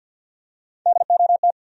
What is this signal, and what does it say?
Morse code, heard as a keyed tone, about 52 wpm, 700 Hz. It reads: DOT
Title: USA Common Names - Max 4 Letters - 50wpm